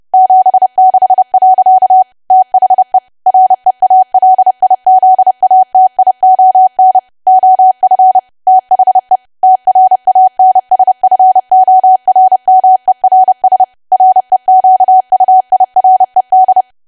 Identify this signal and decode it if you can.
Morse code, heard as a keyed tone, about 30 wpm, 750 Hz. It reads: 76. THE REALIZATION OF THE TRANSFORMERS REQUIRED
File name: cwa / W1AW-Level3 / 30WPM